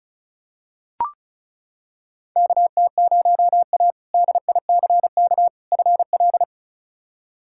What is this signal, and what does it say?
Morse code, heard as a keyed tone, about 36 wpm, 700 Hz. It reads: KT0A DICK FL
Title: K1USN SST QSO Practice - 35wpm